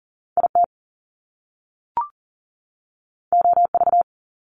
Morse code, heard as a keyed tone, about 38 wpm, 700 Hz. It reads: IT OV
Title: Top Two Letter Combinations in Words 40wpm